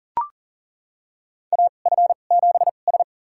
Morse code, heard as a keyed tone, about 37 wpm, spaced as for 28 wpm, 700 Hz. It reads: AF7S
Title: USA Call Signs - 2x Character Spacing - 40wpm